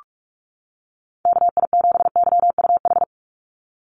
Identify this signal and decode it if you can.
Morse code, heard as a keyed tone, about 46 wpm, 700 Hz. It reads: KI7YUH
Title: USA Call Signs 45wpm